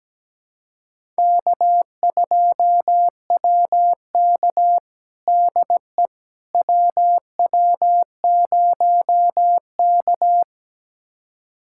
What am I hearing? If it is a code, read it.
Morse code, heard as a keyed tone, about 17 wpm, 700 Hz. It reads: K2WK DE WW0K
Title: USA Callsign DE Callsign - 4 Characters - 17wpm